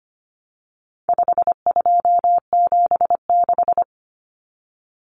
Morse code, heard as a keyed tone, about 25 wpm, 700 Hz. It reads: 5276
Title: Numbers - 4-Digit 25wpm